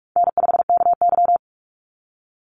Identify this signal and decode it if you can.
Morse code, heard as a keyed tone, about 46 wpm, 700 Hz. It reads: N5KY
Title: Call Signs - 45wpm